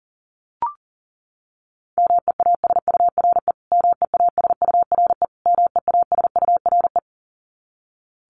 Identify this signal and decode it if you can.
Morse code, heard as a keyed tone, about 40 wpm, 700 Hz. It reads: MEASURE MEASURE MEASURE
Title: Top 500 Words Repeated 40wpm